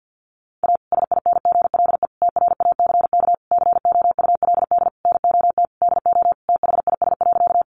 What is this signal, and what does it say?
Morse code, heard as a keyed tone, about 50 wpm, 700 Hz. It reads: A SINGLE TRACK COULD NOT DO THIS.
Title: Sentences from Top 1100 Words 50wpm